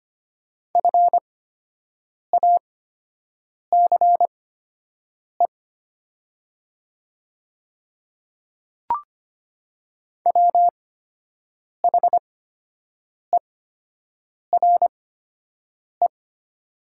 Morse code, heard as a keyed tone, about 25 wpm, spaced as for 7 wpm, 700 Hz. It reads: FACE WHERE
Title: 8x Character Spacing - Top 300 Words - 25wpm